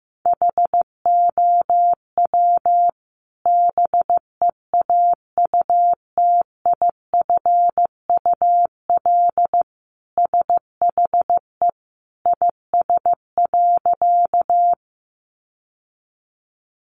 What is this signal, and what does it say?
Morse code, heard as a keyed tone, about 14 wpm, 700 Hz. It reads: HOW BEAUTIFUL SHE IS.
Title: Sentences from Top 900 Words 15wpm